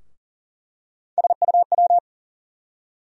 Morse code, heard as a keyed tone, about 40 wpm, 700 Hz. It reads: SUW
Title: USA Call Sign Postfixes - 3 Letters 40wpm